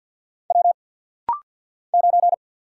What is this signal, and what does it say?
Morse code, heard as a keyed tone, about 50 wpm, 700 Hz. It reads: W 9
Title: Single Letter-Number - Rapid-Fire 50wpm